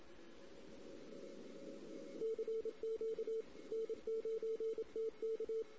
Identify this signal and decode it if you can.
Morse code, heard as a keyed tone, about 27 wpm, 450 Hz. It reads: CQ N9TK
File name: CW / 2008-ss-cw-k5tr-op-n5rz -16:00